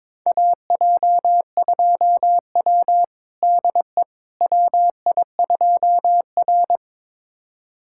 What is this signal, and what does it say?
Morse code, heard as a keyed tone, about 22 wpm, 700 Hz. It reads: AJ2W DE WI2R
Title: USA Callsign DE Callsign - 4 Characters - 22wpm